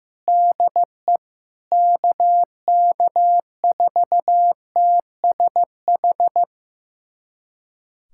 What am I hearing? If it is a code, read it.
Morse code, heard as a keyed tone, about 15 wpm, 700 Hz. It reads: DE KK4TSH
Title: USA Callsign DE Callsign - 6 Characters - 15wpm